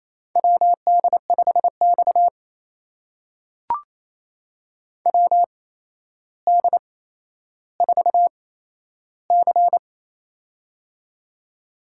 Morse code, heard as a keyed tone, about 28 wpm, spaced as for 12 wpm, 700 Hz. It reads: WD5X WD4C
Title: USA Call Signs - 4 Characters - 8x Character Spacing - 28wpm